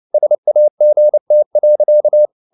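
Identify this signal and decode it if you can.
Morse code, heard as a keyed tone, about 29 wpm, 600 Hz. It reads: SAGT.
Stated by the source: Fast (QRQ) Morse at 29 wpm, from CW med Gnister.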